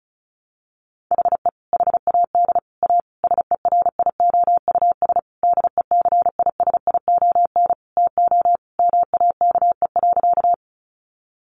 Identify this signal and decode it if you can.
Morse code, heard as a keyed tone, about 34 wpm, 700 Hz. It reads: HE HAD A SERIOUS DECISION TO MAKE.
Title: Sentences from Top 1100 Words 35wpm